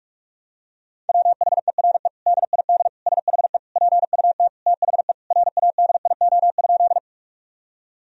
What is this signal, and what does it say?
Morse code, heard as a keyed tone, about 41 wpm, 700 Hz. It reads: WHERE DID SHE PUT THE RADIO?
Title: Sentences from Top 1000 Words - Review Part 3 - 45wpm